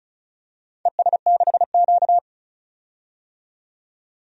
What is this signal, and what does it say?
Morse code, heard as a keyed tone, about 36 wpm, 700 Hz. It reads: ES6Q
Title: DXCC-100 Call Signs - 4 Characters - 35wpm